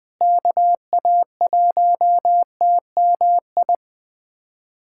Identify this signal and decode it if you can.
Morse code, heard as a keyed tone, about 21 wpm, 700 Hz. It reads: KA1TMI
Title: USA Callsign DE Callsign - 6 Characters - 20wpm